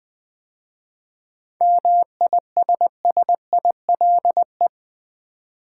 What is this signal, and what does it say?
Morse code, heard as a keyed tone, about 20 wpm, 700 Hz. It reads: MISSILE
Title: 7 Letter Words - First 3 Letters Given - 20wpm